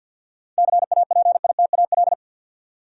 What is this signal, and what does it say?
Morse code, heard as a keyed tone, about 50 wpm, 700 Hz. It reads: CAPITAL
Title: Sentences from Top 1000 Words - New Word Drill - 50wpm